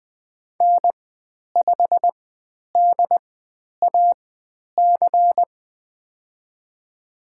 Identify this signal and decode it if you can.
Morse code, heard as a keyed tone, about 20 wpm, spaced as for 10 wpm, 700 Hz. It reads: N5DAC
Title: 095 Lesson 16g - Letter c - Call Signs 20wpm